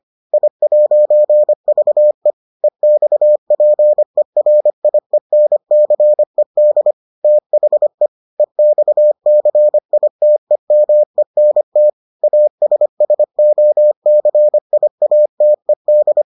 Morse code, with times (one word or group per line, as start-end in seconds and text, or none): 0.3-2.3 I'VE
2.6-6.9 EXPERIENCED
7.2-8.1 THE
8.4-11.9 EXCITEMENT
12.2-16.2 ASSOCIATED